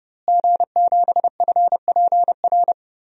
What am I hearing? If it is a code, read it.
Morse code, heard as a keyed tone, about 32 wpm, 700 Hz. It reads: G7FPR
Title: DXCC-100 Call Signs - 5 Characters - Repeated 2x - 30wpm